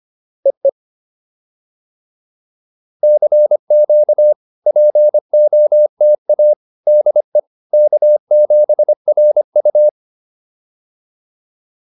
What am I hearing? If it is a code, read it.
Morse code, heard as a keyed tone, about 25 wpm, 590 Hz. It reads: EE CQ POTA DE K7RU